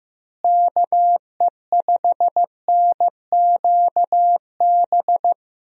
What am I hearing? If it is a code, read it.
Morse code, heard as a keyed tone, about 15 wpm, 700 Hz. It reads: KE5NQB
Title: USA Call Signs - 6 Characters - Repeated 2x - 15wpm